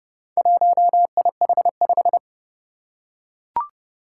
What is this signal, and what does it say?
Morse code, heard as a keyed tone, about 30 wpm, 700 Hz. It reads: 1IH5
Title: Four Challenging Letter-Number Combinations 30wpm